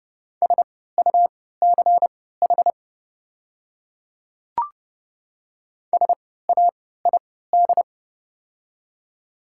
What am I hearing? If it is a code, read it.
Morse code, heard as a keyed tone, about 31 wpm, spaced as for 17 wpm, 700 Hz. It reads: SUCH SAID
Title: 3x Character Spacing - Top 300 Words - 30wpm